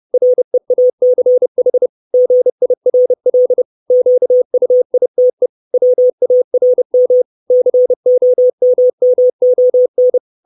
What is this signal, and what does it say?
Morse code, heard as a keyed tone, about 30 wpm, 500 Hz. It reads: REACH GIRL QUITE WARM COMMON
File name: 632 Words 30WPM